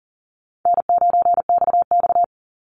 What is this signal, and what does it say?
Morse code, heard as a keyed tone, about 40 wpm, 700 Hz. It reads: N9XX
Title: Call Signs Repeated 2x - 40wpm